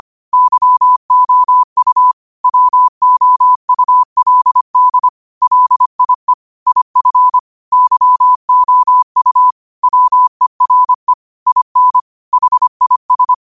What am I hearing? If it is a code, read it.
Morse code, heard as a keyed tone, about 25 wpm, 1000 Hz. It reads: YOU WOULD LIE IF YOU WERE IN HIS